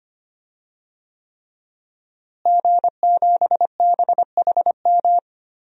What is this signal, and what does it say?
Morse code, heard as a keyed tone, about 25 wpm, 700 Hz. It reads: G7BHM
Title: DXCC-100 Call Signs - 5 Characters - Repeated 3x - 25wpm